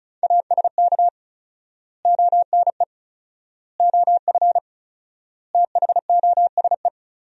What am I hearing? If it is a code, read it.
Morse code, heard as a keyed tone, about 36 wpm, 700 Hz. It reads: ASK ONE OF THOSE
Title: 4x Word-Spacing - Sentences from Top 300 Words - 35wpm